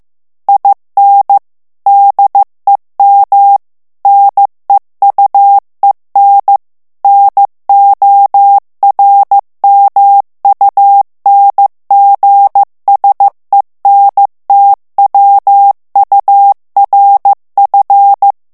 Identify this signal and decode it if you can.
Morse code, heard as a keyed tone, about 15 wpm, 800 Hz. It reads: IN DEM NEUEN NORMUNGSENTWURF
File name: morse mp3 / ALLEMAND texte clair 1 vitesse 14 mots minute